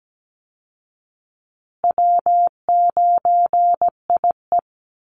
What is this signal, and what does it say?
Morse code, heard as a keyed tone, about 17 wpm, 700 Hz. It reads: W9IE
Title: Easy Call Signs Repeated 3x - 17wpm